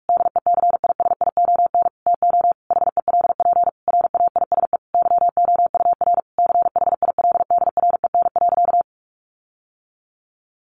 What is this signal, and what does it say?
Morse code, heard as a keyed tone, about 44 wpm, 700 Hz. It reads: DECISION TO HELP RAISE YOUR CHILDREN.
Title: Sentences from Top 1400 Words 45wpm